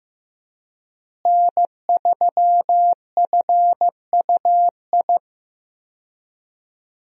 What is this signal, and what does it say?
Morse code, heard as a keyed tone, about 15 wpm, 700 Hz. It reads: N3FUI
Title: USA Call Signs - 5 Characters - Repeated 2x - 15wpm